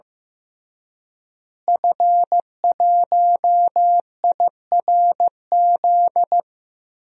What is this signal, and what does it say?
Morse code, heard as a keyed tone, about 15 wpm, 700 Hz. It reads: F1IRZ
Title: DXCC-100 Call Signs - 5 Characters - Repeated 3x - 15wpm